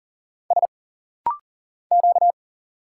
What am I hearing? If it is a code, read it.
Morse code, heard as a keyed tone, about 40 wpm, 700 Hz. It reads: S Q
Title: Single Letter-Number Alt - Rapid-Fire 40wpm